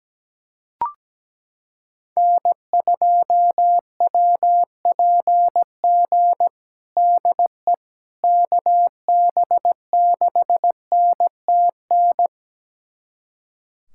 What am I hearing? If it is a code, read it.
Morse code, heard as a keyed tone, about 17 wpm, 700 Hz. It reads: N2WPG DE KB6NTN
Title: USA Callsign DE Callsign 17wpm